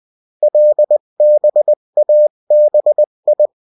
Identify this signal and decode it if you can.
Morse code, heard as a keyed tone, about 20 wpm, 600 Hz. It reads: LBABI
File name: morse mp3 / revision9-vitesse 18 mots